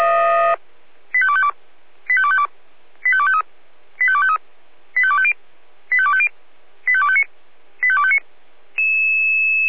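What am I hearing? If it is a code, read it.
Five-tone selective call (ZVEI), heard as quick runs of five tones, tones of about 70 ms. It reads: 85252 85252 85253 85253 85250 85250 85259 85259
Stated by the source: ZVEI five-tone selective call received on SDR.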